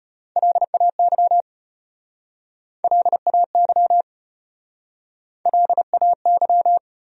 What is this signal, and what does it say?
Morse code, heard as a keyed tone, about 34 wpm, 700 Hz. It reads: LAY LAY LAY